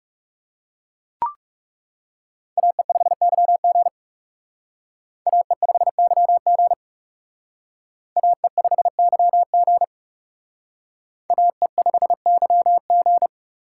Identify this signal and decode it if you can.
Morse code, heard as a keyed tone, about 38 wpm, 700 Hz. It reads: AE5YG AE5YG AE5YG AE5YG